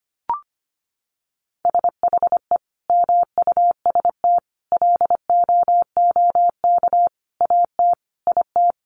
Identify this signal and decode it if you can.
Morse code, heard as a keyed tone, about 25 wpm, 700 Hz. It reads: SHE MUST LOOK AT IT
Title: Sentences from Top 1400 Words 25wpm